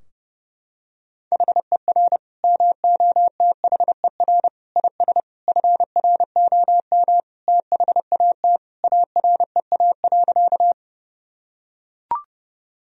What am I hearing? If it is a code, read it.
Morse code, heard as a keyed tone, about 31 wpm, 700 Hz. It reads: HER MOTHER IS FROM THAT AREA.
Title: Sentences from Top 1000 Words - Review Part 2 - 30wpm